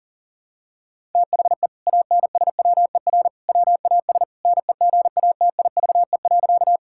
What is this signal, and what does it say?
Morse code, heard as a keyed tone, about 40 wpm, 700 Hz. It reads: THE ANSWER WAS NEGATIVE.
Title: Sentences from Top 1000 Words - Review Part 6 - 40wpm